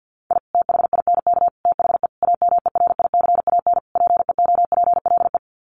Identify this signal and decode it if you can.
Morse code, heard as a keyed tone, about 49 wpm, 700 Hz. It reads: I THINK THE AMERICAN PEOPLE
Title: Sets of 5 Words Encore 50wpm